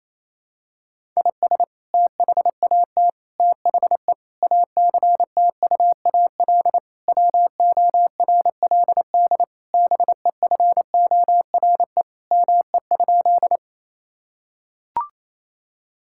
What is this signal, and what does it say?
Morse code, heard as a keyed tone, about 28 wpm, 700 Hz. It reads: IS THAT THE ACTUAL WORLD BEFORE ME?
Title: Sentences from Top 1000 Words - Review Part 5 - 28wpm